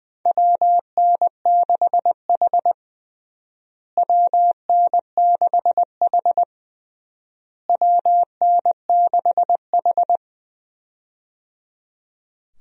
Morse code, heard as a keyed tone, about 20 wpm, 700 Hz. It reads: WN6H WN6H WN6H
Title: USA Call Signs - 4 Characters - Repeated 3x - 20wpm